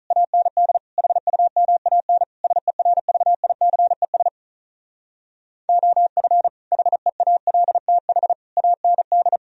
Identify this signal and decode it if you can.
Morse code, heard as a keyed tone, about 46 wpm, 700 Hz. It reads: AND HUMAN SERVICES OF HEALTH AND